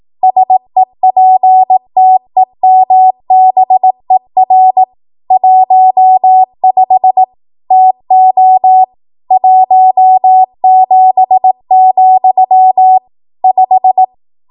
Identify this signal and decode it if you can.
Morse code, heard as a keyed tone, about 18 wpm, 750 Hz. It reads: SEPTEMBER 15 TO 17, 5
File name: Archive / 18 WPM / 18WPM